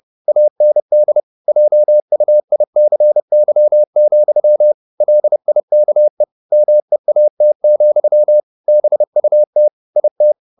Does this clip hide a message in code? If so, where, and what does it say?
Yes: Morse code from 0.3 to 10.3 s, heard as a keyed tone, about 30 wpm, 600 Hz. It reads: AND JUICY, LIKE MEAT, BUT IT